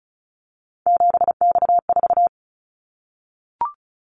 Morse code, heard as a keyed tone, about 36 wpm, 700 Hz. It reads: 7X4
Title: Callsign Prefixes for Top 100 DXCC - 35wpm